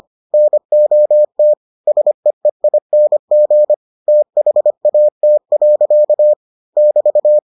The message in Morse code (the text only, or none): NOT SEEING THAT. <BT>